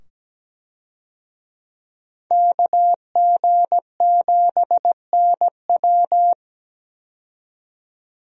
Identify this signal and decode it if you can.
Morse code, heard as a keyed tone, about 17 wpm, 700 Hz. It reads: KG7NW
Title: USA Call Signs - 5 Characters - 17wpm